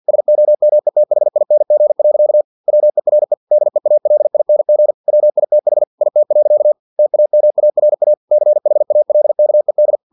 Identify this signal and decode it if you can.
Morse code, heard as a keyed tone, about 50 wpm, 600 Hz. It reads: SOMETHING. WERE DEALING WITH IT. TAMARA CHALKED
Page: Never 49wpm